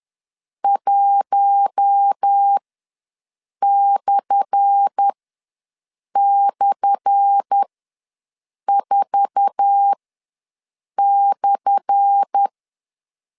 Morse code, heard as a keyed tone, about 10 wpm, spaced as for 6 wpm, 800 Hz. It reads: 1//4/
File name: LEZIONI / 10 SIGNS / 10A SLASH ws 10wpm 4121